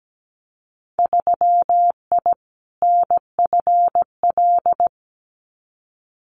Morse code, heard as a keyed tone, about 17 wpm, 700 Hz. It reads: 3I NFL
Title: Winter Field Day Exchanges 17wpm